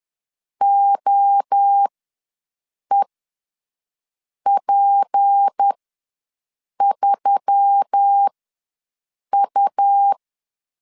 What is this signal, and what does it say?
Morse code, heard as a keyed tone, about 10 wpm, spaced as for 6 wpm, 800 Hz. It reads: OEP3U
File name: LEZIONI / 6 PKX0 / 6 PK ws 10wpm 3745